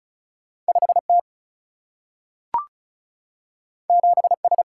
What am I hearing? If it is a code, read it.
Morse code, heard as a keyed tone, about 36 wpm, 700 Hz. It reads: 5 T 7 S
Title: Single-Letter Single-Number 35wpm